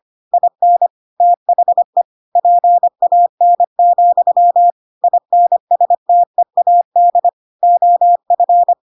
Morse code, heard as a keyed tone, about 25 wpm, 700 Hz. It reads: IN THE PAN, INSTEAD OF
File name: Pinocchio / Pinocchio ch5 25 25 ik1hge